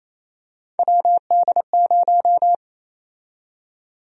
Morse code, heard as a keyed tone, about 27 wpm, 700 Hz. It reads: WD0
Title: USA Call Sign Prefixes - 28wpm